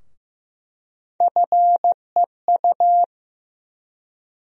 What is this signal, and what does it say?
Morse code, heard as a keyed tone, about 15 wpm, 700 Hz. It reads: FEU
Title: USA Call Sign Postfixes - 3 Letters 15wpm